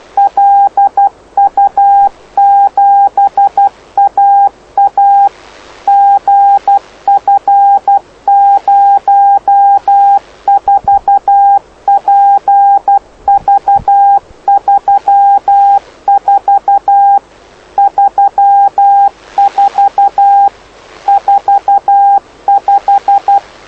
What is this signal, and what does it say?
Morse code, heard as a keyed tone, about 12 wpm, 800 Hz. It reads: LU7AA GF04PV34 34 45
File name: lu4aao / lu7aa 09dic2011 exp planeador vuelo3